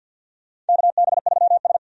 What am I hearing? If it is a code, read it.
Morse code, heard as a keyed tone, about 52 wpm, 700 Hz. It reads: KB3S
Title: USA Call Signs - 4 Characters - 50wpm